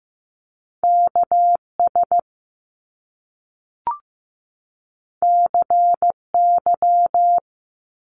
Morse code, heard as a keyed tone, about 15 wpm, 700 Hz. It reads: KS CY
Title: Top Two Letter Combinations in Words 15wpm